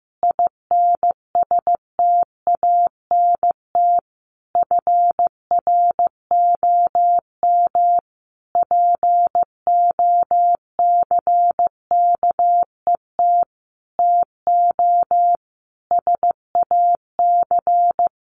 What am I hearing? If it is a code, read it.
Morse code, heard as a keyed tone, about 15 wpm, 700 Hz. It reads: INSTANT FROM POCKET TO SAC
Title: A Sound Of Thunder 15wpm